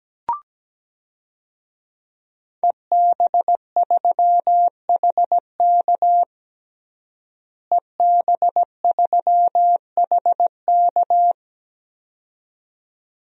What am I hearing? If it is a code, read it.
Morse code, heard as a keyed tone, about 17 wpm, 700 Hz. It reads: EB3HK EB3HK
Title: DXCC-100 Call Signs - 5 Characters - Repeated 2x - 17wpm